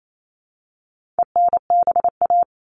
Morse code, heard as a keyed tone, about 28 wpm, 700 Hz. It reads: ENBA
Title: All Cut Numbers - 4-Digit 28wpm